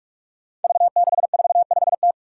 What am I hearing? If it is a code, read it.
Morse code, heard as a keyed tone, about 44 wpm, 700 Hz. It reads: V645T
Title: Five Challenging Letter-Number Combinations 45wpm